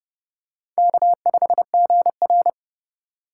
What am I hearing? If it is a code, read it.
Morse code, heard as a keyed tone, about 32 wpm, 700 Hz. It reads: K5GR
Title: USA Call Signs - 4 Characters - 2x Character Spacing - 30wpm